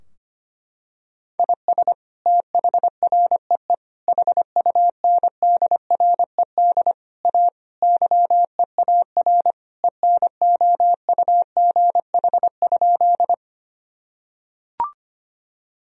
Morse code, heard as a keyed tone, about 25 wpm, 700 Hz. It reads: IS THREE HUNDRED A YEAR ENOUGH?
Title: Sentences from Top 1000 Words - Review Part 2 - 25wpm